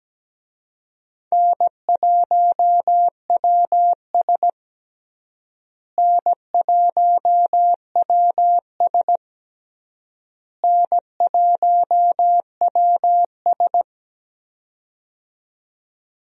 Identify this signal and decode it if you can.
Morse code, heard as a keyed tone, about 17 wpm, 700 Hz. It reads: N1WS N1WS N1WS
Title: USA Call Signs - 4 Characters - Repeated 3x - 17wpm